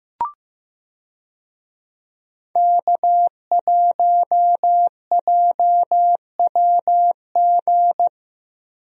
Morse code, heard as a keyed tone, about 15 wpm, 700 Hz. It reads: K1JWG
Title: USA Call Signs - 5 Characters - Repeated 3x - 15wpm